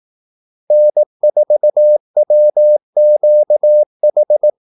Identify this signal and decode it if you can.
Morse code, heard as a keyed tone, about 18 wpm, 600 Hz. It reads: N4WQH
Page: Kaldesignaler 18 wpm